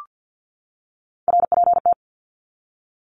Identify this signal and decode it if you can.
Morse code, heard as a keyed tone, about 50 wpm, 700 Hz. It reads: RPT
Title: Abbreviations 50wpm